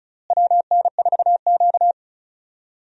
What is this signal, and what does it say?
Morse code, heard as a keyed tone, about 36 wpm, 700 Hz. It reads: WN4Q
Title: USA Call Signs - 4 Characters - Repeated 3x - 35wpm